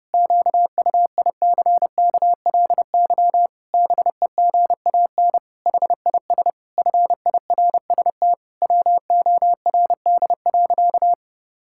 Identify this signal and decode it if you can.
Morse code, heard as a keyed tone, about 30 wpm, 700 Hz. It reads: QUICKLY BEGAN HIS FIRST WORD.
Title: Pangram Challenge v2 - 30wpm